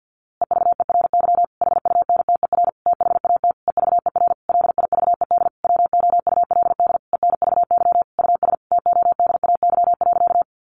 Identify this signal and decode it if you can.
Morse code, heard as a keyed tone, about 50 wpm, 700 Hz. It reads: EVERY HUNTER THAT EVER LIVED WOULD ENVY US TODAY.
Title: A Sound Of Thunder 50wpm